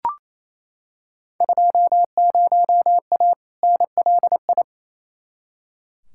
Morse code, heard as a keyed tone, about 28 wpm, 700 Hz. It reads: 20A NLI
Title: Field Day Exchanges Encore - 28wpm